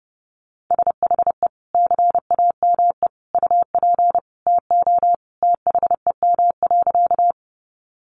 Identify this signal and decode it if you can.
Morse code, heard as a keyed tone, about 30 wpm, 700 Hz. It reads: SHE CAME UP TO THEM.
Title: Sentences from Top 200 Words 30wpm